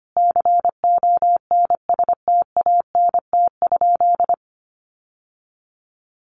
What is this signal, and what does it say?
Morse code, heard as a keyed tone, about 25 wpm, 700 Hz. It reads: CONSTANT?
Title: Sentences from Top 1500 Words 25wpm